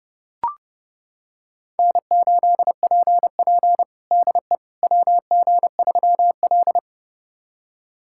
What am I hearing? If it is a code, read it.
Morse code, heard as a keyed tone, about 30 wpm, 700 Hz. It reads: N8PP DE WG3L
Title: USA Callsign DE Callsign - 4 Characters - 30wpm